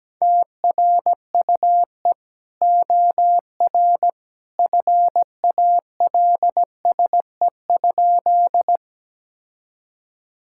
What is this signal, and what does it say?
Morse code, heard as a keyed tone, about 17 wpm, 700 Hz. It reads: TRUE OR FALSE?
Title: Sentences from Top 1600 Words 17wpm